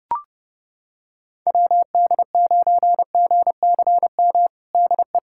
Morse code, heard as a keyed tone, about 30 wpm, 700 Hz. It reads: WD9GCM DE
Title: USA Callsign DE Callsign - 6 Characters - 30wpm